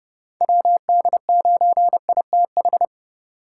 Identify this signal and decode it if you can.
Morse code, heard as a keyed tone, about 29 wpm, 700 Hz. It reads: WD9ITH
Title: USA Call Signs - 4x Character Spacing - 30wpm